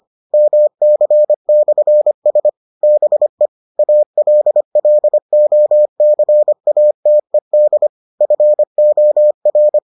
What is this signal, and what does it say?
Morse code, heard as a keyed tone, about 25 wpm, 600 Hz. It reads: MC/S BE ALLOCATED FOR